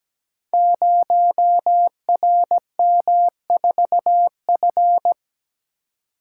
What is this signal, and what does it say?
Morse code, heard as a keyed tone, about 17 wpm, 700 Hz. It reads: 0RM4F
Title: Code Groups - 5 Letter:Numbers - 17wpm